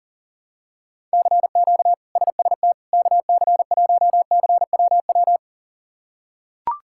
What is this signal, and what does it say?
Morse code, heard as a keyed tone, about 40 wpm, 700 Hz. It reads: CQ SST KC1CWW
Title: K1USN SST QSO Practice - 40wpm